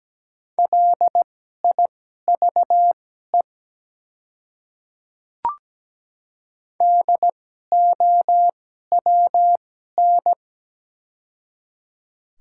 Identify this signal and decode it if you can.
Morse code, heard as a keyed tone, about 17 wpm, spaced as for 12 wpm, 700 Hz. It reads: LIVE DOWN
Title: 2x Character Spacing - Top 200 Words - 17wpm